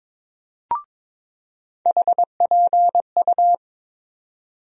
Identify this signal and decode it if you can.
Morse code, heard as a keyed tone, about 22 wpm, 700 Hz. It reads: HPU
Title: USA Call Sign Postfixes - 3 Letters 22wpm